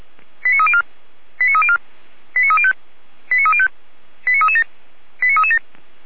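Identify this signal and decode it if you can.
Five-tone selective call (ZVEI), heard as quick runs of five tones, tones of about 70 ms. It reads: 89304 89304 89305 89305 89337 89337
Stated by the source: ZVEI five-tone selective call received on SDR, fire brigade Feuerwehr Orlamünde.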